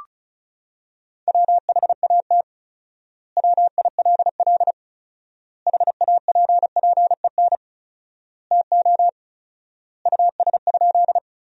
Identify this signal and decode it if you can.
Morse code, heard as a keyed tone, about 36 wpm, 700 Hz. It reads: WHAT WILL HAPPEN TO US?
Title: 4x Word-Spacing - Sentences from Top 400 Words - 35wpm